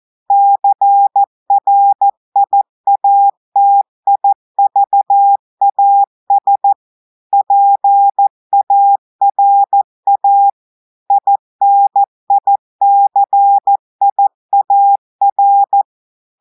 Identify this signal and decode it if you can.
Morse code, heard as a keyed tone, about 14 wpm, 800 Hz. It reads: CRIATIVAS PARA INICIAR
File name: audios / portugais / portugais texte en clair 2 vitesse 14 mots minutes